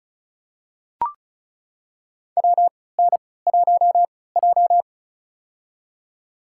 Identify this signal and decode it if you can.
Morse code, heard as a keyed tone, about 36 wpm, spaced as for 20 wpm, 700 Hz. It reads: WN1J
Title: USA Call Signs - 4 Characters - 3x Character Spacing - 35wpm